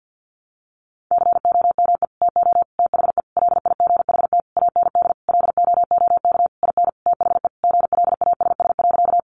Morse code, heard as a keyed tone, about 50 wpm, 700 Hz. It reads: COME TO THE LIGHT AND LOOK IN THE GLASS.
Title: Sentences from Top 1300 Words 50wpm